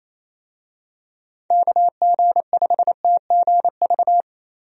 Morse code, heard as a keyed tone, about 27 wpm, 700 Hz. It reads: KG5TGV
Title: USA Call Signs - 6 Characters - Repeated 3x - 28wpm